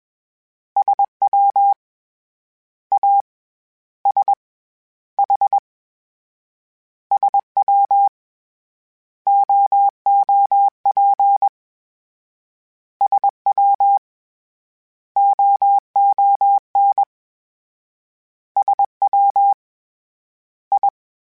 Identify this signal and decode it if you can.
Morse code, heard as a keyed tone, about 21 wpm, 800 Hz. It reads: SW A S H SW OOP SW OON SW I